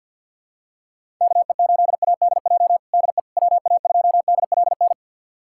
Morse code, heard as a keyed tone, about 52 wpm, 700 Hz. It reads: KE8ADJ DE WA2DLN
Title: USA Callsign DE Callsign - 6 Characters - 50wpm